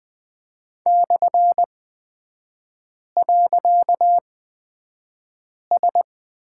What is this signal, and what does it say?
Morse code, heard as a keyed tone, about 20 wpm, spaced as for 5 wpm, 700 Hz. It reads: /.S
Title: 255 Lesson 41e - Period - Three Characters 20wpm